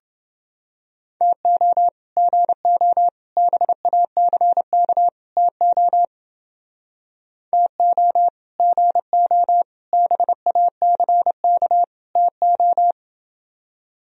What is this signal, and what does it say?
Morse code, heard as a keyed tone, about 30 wpm, 700 Hz. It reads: TO GO BACK TO TO GO BACK TO